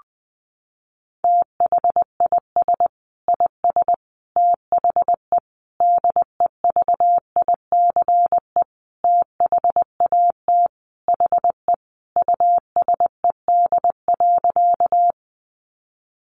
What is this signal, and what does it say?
Morse code, heard as a keyed tone, about 20 wpm, 700 Hz. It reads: THIS IS THE DEVICE THAT HE USED.
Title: Sentences from Top 1500 Words 20wpm